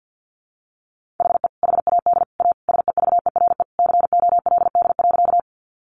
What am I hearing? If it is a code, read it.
Morse code, heard as a keyed tone, about 50 wpm, 700 Hz. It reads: HE HAD A SEVERE COLD.
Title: Sentences from Top 1400 Words 50wpm